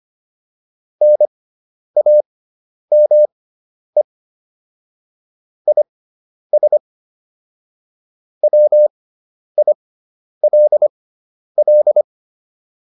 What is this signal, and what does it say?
Morse code, heard as a keyed tone, about 25 wpm, spaced as for 10 wpm, 600 Hz. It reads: NAME IS WILL
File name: QSO409 10